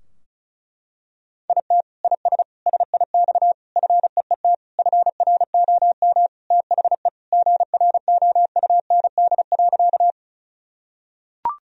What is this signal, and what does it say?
Morse code, heard as a keyed tone, about 34 wpm, 700 Hz. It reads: IT IS SIX FEET FROM THE GROUND.
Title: Sentences from Top 1000 Words - Review Part 2 - 35wpm